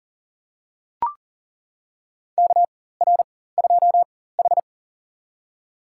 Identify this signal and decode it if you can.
Morse code, heard as a keyed tone, about 40 wpm, spaced as for 19 wpm, 700 Hz. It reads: KR2H
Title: USA Call Signs - 4x Character Spacing - 40wpm